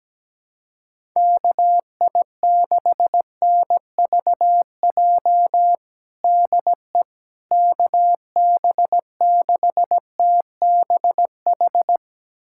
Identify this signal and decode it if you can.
Morse code, heard as a keyed tone, about 17 wpm, 700 Hz. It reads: KI6NVJ DE KB6TBH
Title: USA Callsign DE Callsign - 6 Characters - 17wpm